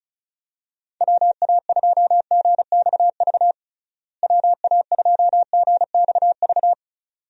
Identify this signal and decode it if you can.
Morse code, heard as a keyed tone, about 36 wpm, 700 Hz. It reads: WA2GXV WA2GXV
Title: USA Call Signs - 6 Characters - Repeated 2x - 35wpm